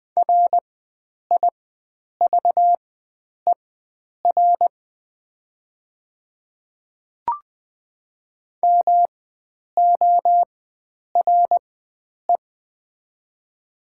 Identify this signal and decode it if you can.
Morse code, heard as a keyed tone, about 20 wpm, spaced as for 9 wpm, 700 Hz. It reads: RIVER MORE
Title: 4x Character Spacing - Top 300 Words - 20wpm